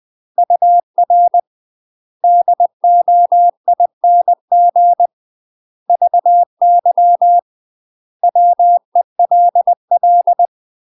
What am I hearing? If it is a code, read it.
Morse code, heard as a keyed tone, about 20 wpm, 700 Hz. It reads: UR DOING VY WELL
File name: UK MASTER QSOs TRAINING / 20WPM / UK MASTER QSOs 0046